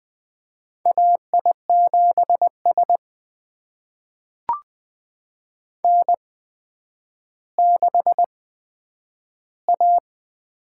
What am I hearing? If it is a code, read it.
Morse code, heard as a keyed tone, about 20 wpm, spaced as for 10 wpm, 700 Hz. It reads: AI7S N6A
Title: USA Call Signs - 4 Characters - 8x Character Spacing - 20wpm